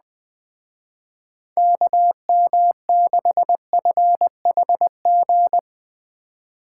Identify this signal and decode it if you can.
Morse code, heard as a keyed tone, about 20 wpm, 700 Hz. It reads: KM6FHG
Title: Speed Racing - USA 2x3 Call Signs 17wpm